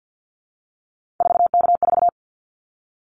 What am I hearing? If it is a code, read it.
Morse code, heard as a keyed tone, about 48 wpm, 700 Hz. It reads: 4K4
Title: Callsign Prefixes for Top 100 DXCC - 50wpm